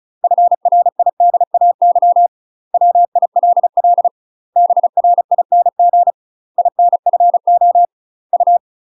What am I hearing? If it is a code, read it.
Morse code, heard as a keyed tone, about 35 wpm, 700 Hz. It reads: FRIDAY WILL BRING INFO U
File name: UK MASTER QSOs TRAINING / 35WPM / UK MASTER QSOs 0014